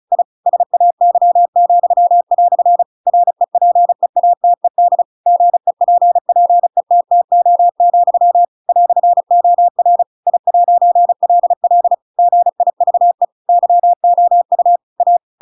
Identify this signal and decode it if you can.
Morse code, heard as a keyed tone, about 34 wpm, 700 Hz. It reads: I SAY," REPEATED GEPPETTO, "OR I'LL GIVE YOU A
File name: Pinocchio / Pinocchio ch7 35 35 ik1hge